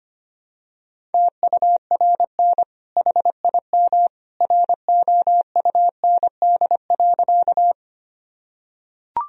Morse code, heard as a keyed tone, about 25 wpm, 700 Hz. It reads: TURN HIM ROUND.
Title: Sentences from Top 1000 Words - Review Part 1 - 25wpm